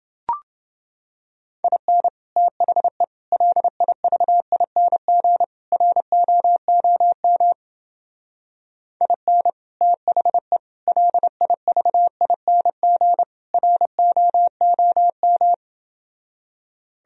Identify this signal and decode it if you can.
Morse code, heard as a keyed tone, about 28 wpm, 700 Hz. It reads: IN THE LIVING ROOM IN THE LIVING ROOM